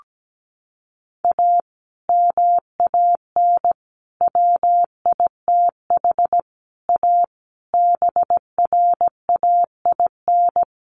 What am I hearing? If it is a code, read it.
Morse code, heard as a keyed tone, about 17 wpm, 700 Hz. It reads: A MAN WITH A BRAIN
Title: Sentences from Top 1100 Words 17wpm